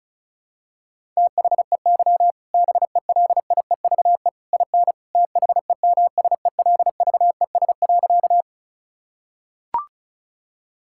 Morse code, heard as a keyed tone, about 34 wpm, 700 Hz. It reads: THEY BELIEVE IN THEMSELVES.
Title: Sentences from Top 1000 Words - Review Part 5 - 35wpm